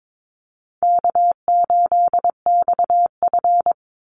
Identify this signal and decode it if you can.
Morse code, heard as a keyed tone, about 22 wpm, 700 Hz. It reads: K8XF
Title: Call Signs Repeated 3x - 22wpm